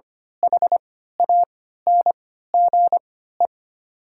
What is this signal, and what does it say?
Morse code, heard as a keyed tone, about 25 wpm, spaced as for 14 wpm, 700 Hz. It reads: HANGE
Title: 3x Character Spacing - Top 500 Words - 25wpm